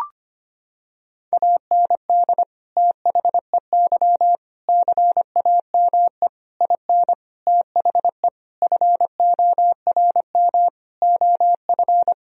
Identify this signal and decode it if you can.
Morse code, heard as a keyed tone, about 25 wpm, 700 Hz. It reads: AND THEY CAME IN THE FORM OF
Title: Sentences from Top 1000 Words - Review Part 1 - 25wpm